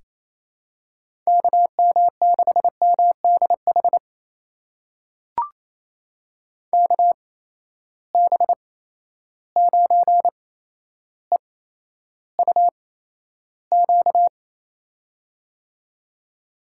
Morse code, heard as a keyed tone, about 28 wpm, spaced as for 12 wpm, 700 Hz. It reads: KM6MDH KB9EUQ
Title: USA Call Signs - 8x Character Spacing - 28wpm